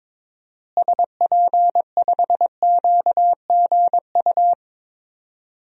Morse code, heard as a keyed tone, about 22 wpm, 700 Hz. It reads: SP5QGU
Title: DXCC-100 Call Signs - 6 Characters - Repeated 3x - 22wpm